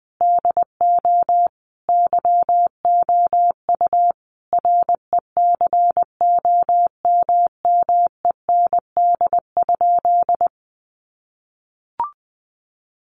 Morse code, heard as a keyed tone, about 19 wpm, 700 Hz. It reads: DO YOU RECOMMEND?
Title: Sentences from Top 1200 Words 20wpm